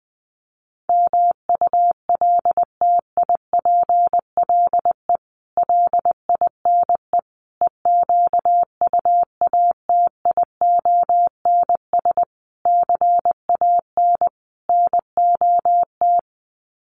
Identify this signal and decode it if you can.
Morse code, heard as a keyed tone, about 20 wpm, 700 Hz. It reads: MULTIPLE LINE EQUATIONS CAN NOT
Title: Sets of 5 Words 20wpm